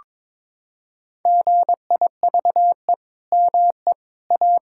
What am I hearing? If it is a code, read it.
Morse code, heard as a keyed tone, about 21 wpm, 700 Hz. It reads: GIVE ME A
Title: Sentences from Top 1000 Words - Review Part 5 - 22wpm